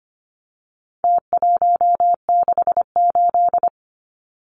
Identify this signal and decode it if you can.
Morse code, heard as a keyed tone, about 25 wpm, 700 Hz. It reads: T168
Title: Numbers - Contest Style - 4-Digit 25wpm